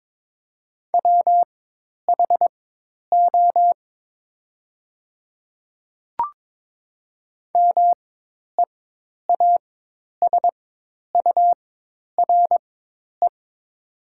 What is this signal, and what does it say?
Morse code, heard as a keyed tone, about 22 wpm, spaced as for 10 wpm, 700 Hz. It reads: WHO MEASURE
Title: 4x Character Spacing - Top 500 Words - 22wpm